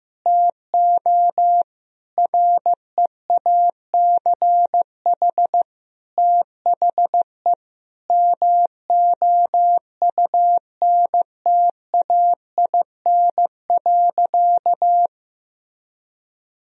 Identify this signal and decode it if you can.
Morse code, heard as a keyed tone, about 15 wpm, 700 Hz. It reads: TO REACH THE MOUNTAIN.
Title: Sentences from Top 1000 Words - Review Part 2 - 15wpm